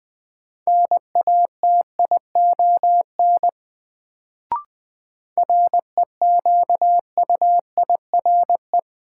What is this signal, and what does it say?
Morse code, heard as a keyed tone, about 21 wpm, 700 Hz. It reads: NATION REQUIRE
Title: Sentences from Top 900 Words - New Word Drill - 20wpm